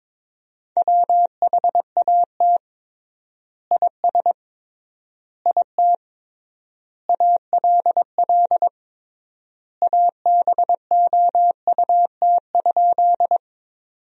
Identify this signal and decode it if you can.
Morse code, heard as a keyed tone, about 22 wpm, 700 Hz. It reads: WHAT IS IT ALL ABOUT?
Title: 3x Word-Spacing - Sentences from Top 100 Words - 22wpm